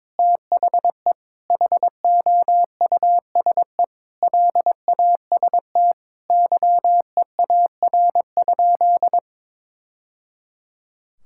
Morse code, heard as a keyed tone, about 21 wpm, 700 Hz. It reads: THE HOUSE LAST YEAR?
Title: Sentences from Top 1000 Words - Review Part 6 - 22wpm